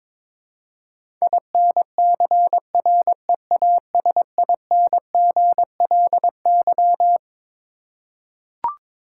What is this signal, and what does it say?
Morse code, heard as a keyed tone, about 22 wpm, 700 Hz. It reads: INCREASINGLY
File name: Twelve Letter Words - 22wpm